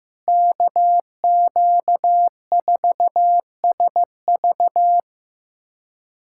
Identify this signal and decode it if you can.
Morse code, heard as a keyed tone, about 15 wpm, 700 Hz. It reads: KQ4SV
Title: USA Callsign DE Callsign - 5 Characters - 15wpm